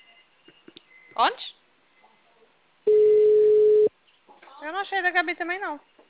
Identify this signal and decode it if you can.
Ringback tone (Europe, 425 Hz), with speech.